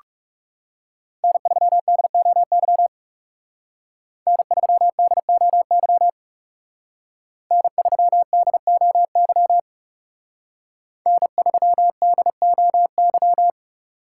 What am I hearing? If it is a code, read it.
Morse code, heard as a keyed tone, about 36 wpm, 700 Hz. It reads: N3DOY N3DOY N3DOY N3DOY
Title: Speed Racing - USA Call Signs 30wpm